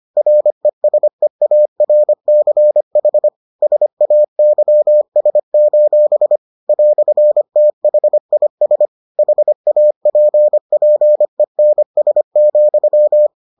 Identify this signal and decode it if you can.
Morse code, heard as a keyed tone, about 25 wpm, 600 Hz. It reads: RESEARCH SAYS: "THIS HAPPENS,